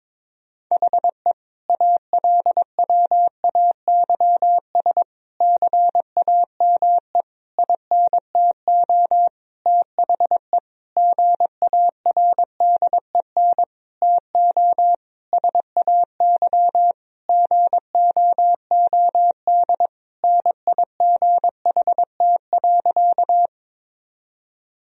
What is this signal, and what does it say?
Morse code, heard as a keyed tone, about 21 wpm, 700 Hz. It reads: HE ALWAYS CAME INTO THE GARDEN TO SAY GOOD NIGHT.
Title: Sentences from Top 1000 Words - Review Part 4 - 22wpm